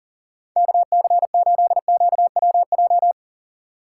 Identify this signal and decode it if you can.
Morse code, heard as a keyed tone, about 46 wpm, 700 Hz. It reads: KC8QWJ
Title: Speed Racing - USA 2x3 Call Signs 40wpm